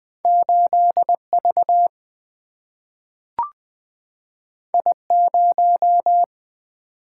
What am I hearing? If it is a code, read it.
Morse code, heard as a keyed tone, about 20 wpm, 700 Hz. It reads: 8V I0
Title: Single-Letter Single-Number 20wpm